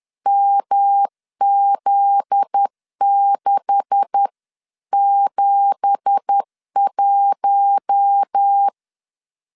Morse code, heard as a keyed tone, about 10 wpm, 800 Hz. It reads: MZ671
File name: LEZIONI / 9 2867 10wpm 2733